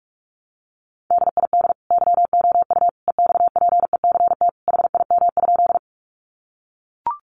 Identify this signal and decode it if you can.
Morse code, heard as a keyed tone, about 44 wpm, 700 Hz. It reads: DID YOU EXPECT HIM?
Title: Sentences from Top 1100 Words 45wpm